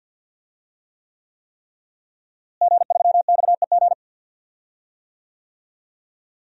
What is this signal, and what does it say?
Morse code, heard as a keyed tone, about 52 wpm, 700 Hz. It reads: G3XEG
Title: DXCC-100 Call Signs - 5 Characters - Repeated 3x - 50wpm